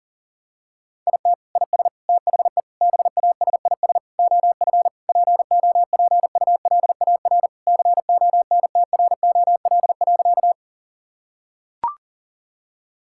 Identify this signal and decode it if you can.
Morse code, heard as a keyed tone, about 37 wpm, 700 Hz. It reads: IT IS THE BASIS OF POPULAR CONTROL.
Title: Sentences from Top 1000 Words - Review Part 5 - 40wpm